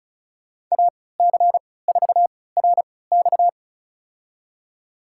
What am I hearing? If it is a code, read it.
Morse code, heard as a keyed tone, about 34 wpm, spaced as for 20 wpm, 700 Hz. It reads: AC4RX
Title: USA Call Signs - 5 Characters - 3x Character Spacing - 35wpm